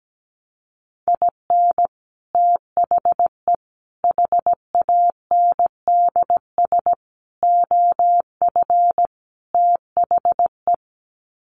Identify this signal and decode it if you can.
Morse code, heard as a keyed tone, about 17 wpm, 700 Hz. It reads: IN THE HANDS OF THE
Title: Sets of 5 Words 17wpm